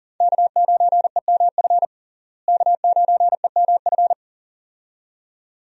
Morse code, heard as a keyed tone, about 37 wpm, 700 Hz. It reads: K9EMF K9EMF
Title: USA Call Signs - 5 Characters - Repeated 2x - 40wpm